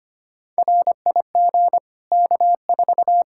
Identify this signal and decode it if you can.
Morse code, heard as a keyed tone, about 25 wpm, 700 Hz. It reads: RIG K4
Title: Practice QSO - Exchange 2 - Info - 25wpm